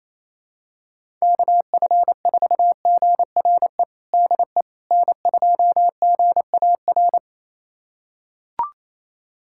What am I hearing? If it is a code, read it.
Morse code, heard as a keyed tone, about 28 wpm, 700 Hz. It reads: KF4GRE DE N2GAR
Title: USA Callsign DE Callsign 28wpm